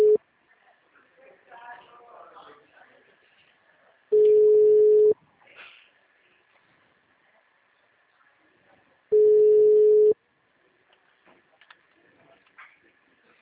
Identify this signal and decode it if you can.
Ringback tone (Europe, 425 Hz).
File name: out- -2012